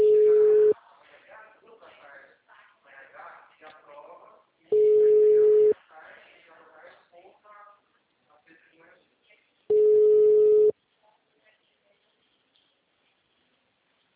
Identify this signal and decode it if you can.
Ringback tone (Europe, 425 Hz), with speech.